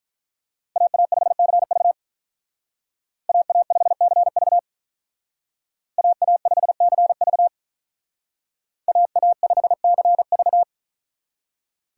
Morse code, heard as a keyed tone, about 43 wpm, 700 Hz. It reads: AA5CV AA5CV AA5CV AA5CV
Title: Speed Racing - USA 2x2 Call Signs 35wpm